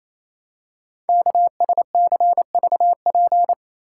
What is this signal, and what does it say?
Morse code, heard as a keyed tone, about 29 wpm, 700 Hz. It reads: KSCVP
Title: Code Groups - 5 Letters - 28wpm